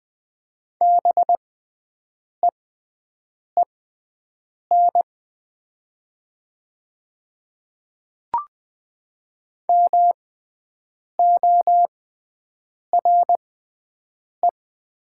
Morse code, heard as a keyed tone, about 20 wpm, spaced as for 7 wpm, 700 Hz. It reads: BEEN MORE
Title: 6x Character Spacing - Top 100 Words - 20wpm